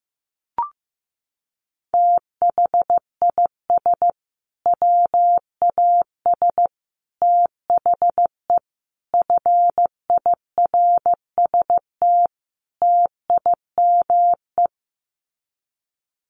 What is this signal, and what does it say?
Morse code, heard as a keyed tone, about 15 wpm, 700 Hz. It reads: THIS WAS THE FIRST TIME
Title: Sets of 5 Words Encore 15wpm